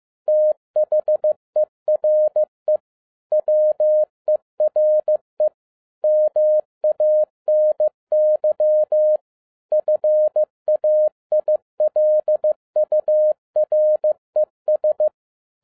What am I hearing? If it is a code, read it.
Morse code, heard as a keyed tone, about 15 wpm, 610 Hz. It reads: THERE WERE MANY FAILURES
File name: 2011-07-mp3 / n57 BACK FROM A BIG ACCIDENT 15wpm